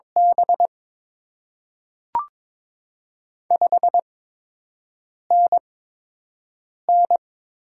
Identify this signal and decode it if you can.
Morse code, heard as a keyed tone, about 22 wpm, spaced as for 6 wpm, 700 Hz. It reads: B 5NN
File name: Top 20 QSO Elements - 8x Character Spacing - 22wpm